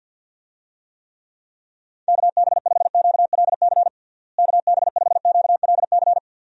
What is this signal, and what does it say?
Morse code, heard as a keyed tone, about 47 wpm, 700 Hz. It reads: KB5QLC KB5QLC
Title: USA Call Signs - 6 Characters - Repeated 2x - 50wpm